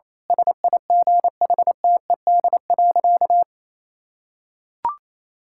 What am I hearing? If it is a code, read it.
Morse code, heard as a keyed tone, about 27 wpm, 700 Hz. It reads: SIGHTED.
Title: Pangram Challenge v2 - 28wpm